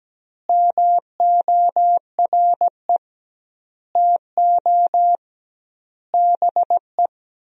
Morse code, heard as a keyed tone, about 17 wpm, 700 Hz. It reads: MORE TO BE
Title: 2x Word-Spacing - Sentences from Top 100 Words - 17wpm